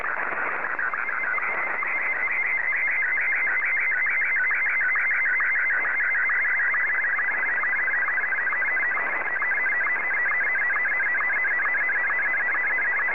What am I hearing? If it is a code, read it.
SSTV (slow-scan television), Martin 1.
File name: M1-VK3